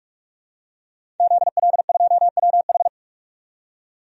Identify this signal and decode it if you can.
Morse code, heard as a keyed tone, about 46 wpm, 700 Hz. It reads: ZL2WH
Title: DXCC-100 Call Signs - 5 Characters - 45wpm